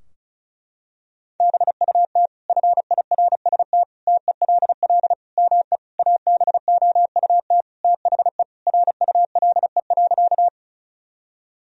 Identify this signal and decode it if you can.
Morse code, heard as a keyed tone, about 34 wpm, 700 Hz. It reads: BUT FIRST TELL ME ABOUT THE RULE.
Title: Sentences from Top 1000 Words - Review Part 2 - 35wpm